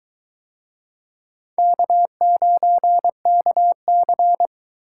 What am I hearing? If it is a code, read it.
Morse code, heard as a keyed tone, about 20 wpm, 700 Hz. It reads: K9KC